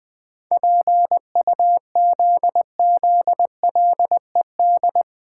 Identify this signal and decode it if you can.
Morse code, heard as a keyed tone, about 20 wpm, 700 Hz. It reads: PUZZLED
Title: Pangram Challenge v2 - 20wpm